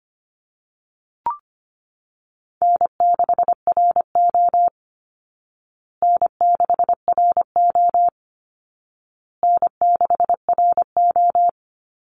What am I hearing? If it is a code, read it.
Morse code, heard as a keyed tone, about 25 wpm, 700 Hz. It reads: N6RO N6RO N6RO
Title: Easy Call Signs Repeated 3x - 25wpm